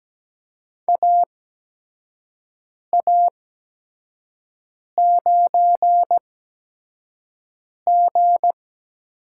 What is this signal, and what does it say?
Morse code, heard as a keyed tone, about 17 wpm, spaced as for 5 wpm, 700 Hz. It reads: AA9G
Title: USA Call Signs - 4 Characters - 8x Character Spacing - 17wpm